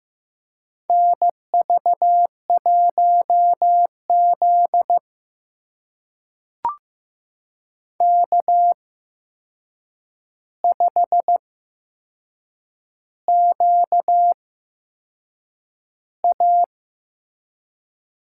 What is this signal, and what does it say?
Morse code, heard as a keyed tone, about 15 wpm, spaced as for 6 wpm, 700 Hz. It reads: NV1Z K5QA
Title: USA Call Signs - 8x Character Spacing - 15wpm